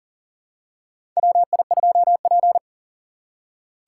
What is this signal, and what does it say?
Morse code, heard as a keyed tone, about 40 wpm, 700 Hz. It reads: WI2P
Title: USA Call Signs - 4 Characters - 4x Character Spacing - 40wpm